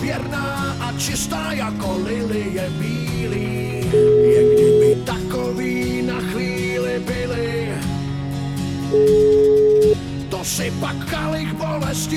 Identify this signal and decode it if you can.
Ringback tone (Europe, 425 Hz), with speech.